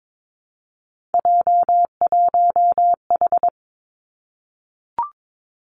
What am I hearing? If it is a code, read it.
Morse code, heard as a keyed tone, about 22 wpm, 700 Hz. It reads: J1H
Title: Three Challenging Letter-Number Combinations 22wpm